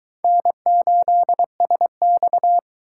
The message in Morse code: N8SX

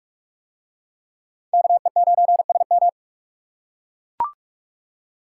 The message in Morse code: KE9SM

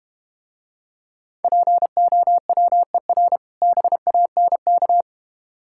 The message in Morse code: POWER BANK

